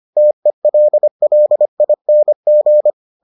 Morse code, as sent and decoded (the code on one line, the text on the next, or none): - . .-.. .-.. .. -. --.
TELLING